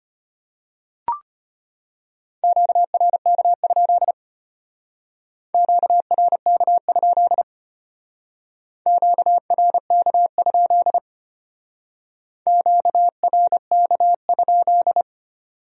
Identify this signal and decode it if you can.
Morse code, heard as a keyed tone, about 31 wpm, 700 Hz. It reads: QRK? QRK? QRK? QRK?